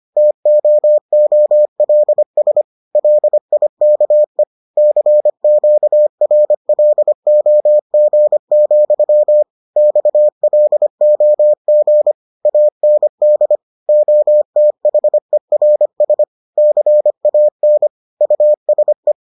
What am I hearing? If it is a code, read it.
Morse code, heard as a keyed tone, about 25 wpm, 600 Hz. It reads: TOOLS LIKE CQRLOG, XLOG AND OTHERS CAN USE